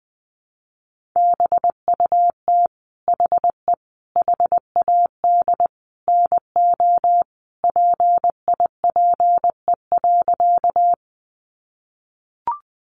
Morse code, heard as a keyed tone, about 20 wpm, 700 Hz. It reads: BUT HE HAD NO PIPE.
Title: Sentences from Top 1500 Words 20wpm